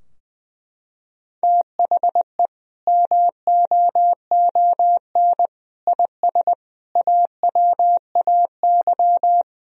Morse code, heard as a keyed tone, about 20 wpm, 700 Hz. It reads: THE MOON IS AWAY
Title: Sentences from Top 1000 Words - Review Part 3 - 20wpm